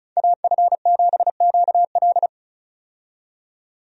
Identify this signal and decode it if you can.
Morse code, heard as a keyed tone, about 36 wpm, 700 Hz. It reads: AF7QL
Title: USA Call Signs - 5 Characters - 35wpm